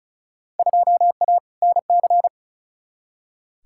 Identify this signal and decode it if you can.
Morse code, heard as a keyed tone, about 36 wpm, 700 Hz. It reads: 2A NC
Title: Field Day Exchanges - 35wpm